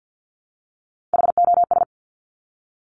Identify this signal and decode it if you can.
Morse code, heard as a keyed tone, about 50 wpm, 700 Hz. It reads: HOS
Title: Top Three Letter Combinations in Words 50wpm